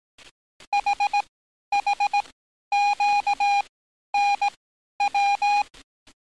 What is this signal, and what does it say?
Morse code, heard as a keyed tone, about 18 wpm, spaced as for 11 wpm, 820 Hz. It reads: HHQNW
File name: training / NAVPERS 11049to11060 / NAVPERS 11050 RA RB